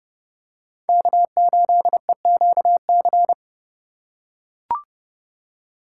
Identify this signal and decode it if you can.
Morse code, heard as a keyed tone, about 29 wpm, 700 Hz. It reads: K8EQC
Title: USA Call Signs - 5 Characters - 6x Character Spacing - 30wpm